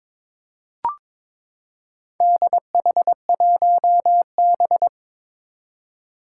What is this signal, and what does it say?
Morse code, heard as a keyed tone, about 22 wpm, 700 Hz. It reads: DH1B
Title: Four Challenging Letter-Number Combinations 22wpm